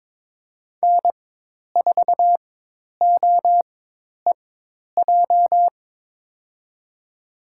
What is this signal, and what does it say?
Morse code, heard as a keyed tone, about 22 wpm, spaced as for 10 wpm, 700 Hz. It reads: N4OEJ
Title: USA Call Signs - 5 Characters - 4x Character Spacing - 22wpm